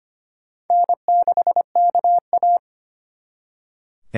Morse code, heard as a keyed tone, about 33 wpm, 700 Hz. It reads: N6KA